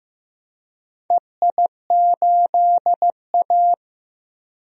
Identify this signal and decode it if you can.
Morse code, heard as a keyed tone, about 15 wpm, 700 Hz. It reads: EI8A
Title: DXCC-100 Call Signs - 4 Characters - Repeated 2x - 15wpm